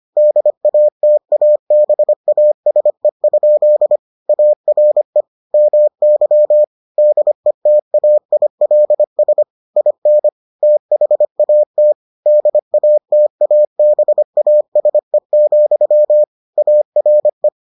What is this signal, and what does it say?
Morse code, heard as a keyed tone, about 25 wpm, 600 Hz. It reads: DATABASE? ARE MY DETAILS IN THAT DATABASE, ARE